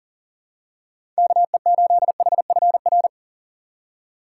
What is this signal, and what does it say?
Morse code, heard as a keyed tone, about 39 wpm, 700 Hz. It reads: KE8HFR
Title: USA Call Signs - 6 Characters - 8x Character Spacing - 40wpm